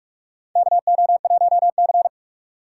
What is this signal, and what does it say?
Morse code, heard as a keyed tone, about 45 wpm, 700 Hz. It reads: KO1C
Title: USA Call Signs - 4 Characters - 4x Character Spacing - 45wpm